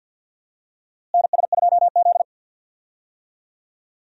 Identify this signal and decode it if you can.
Morse code, heard as a keyed tone, about 50 wpm, 700 Hz. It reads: NS2Z
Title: USA Call Signs - 4 Characters - Repeated 2x - 50wpm